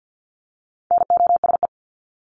Morse code, heard as a keyed tone, about 48 wpm, 700 Hz. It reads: NOSE
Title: Top 1501-1750 Words 50wpm